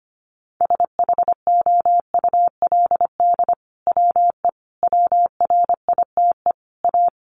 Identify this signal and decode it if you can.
Morse code, heard as a keyed tone, about 25 wpm, 700 Hz. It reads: SHOULD WE WRITE A
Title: Sentences from Top 1300 Words 25wpm